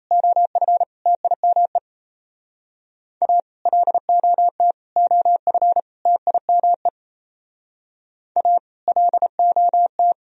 Morse code, heard as a keyed tone, about 35 wpm, 700 Hz. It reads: OF TIME A LOT OF TIME A LOT
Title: Speed-Racing - Sets of 4 Words 28wpm